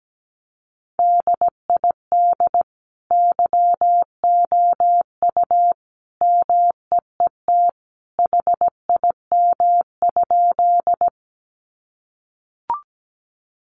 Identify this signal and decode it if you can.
Morse code, heard as a keyed tone, about 17 wpm, 700 Hz. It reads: DID YOU MEET HIM?
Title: Sentences from Top 1100 Words 17wpm